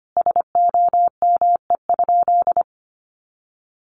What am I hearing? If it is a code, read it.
Morse code, heard as a keyed tone, about 25 wpm, 700 Hz. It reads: SOME?
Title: Sentences from Top 100 Words 25wpm